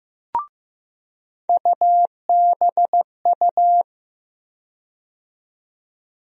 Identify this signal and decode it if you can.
Morse code, heard as a keyed tone, about 15 wpm, 700 Hz. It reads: UBU
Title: USA Call Sign Postfixes - 3 Letters 15wpm